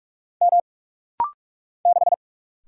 Morse code, heard as a keyed tone, about 46 wpm, 700 Hz. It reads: M 6
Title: Single Letter-Number - Rapid-Fire 45wpm